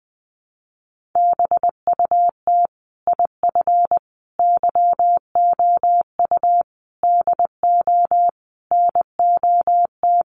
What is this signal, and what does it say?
Morse code, heard as a keyed tone, about 20 wpm, 700 Hz. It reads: BUT IF YOU DO NOT
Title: Sets of 5 Words Encore 20wpm